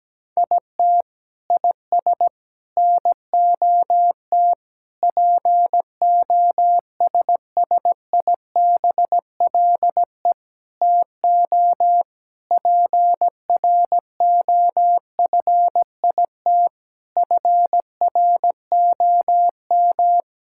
Morse code, heard as a keyed tone, about 17 wpm, 700 Hz. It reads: IT IS NOT POSSIBLE TO PROFIT FROM
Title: Sentences from Top 1000 Words - Review Part 5 - 17wpm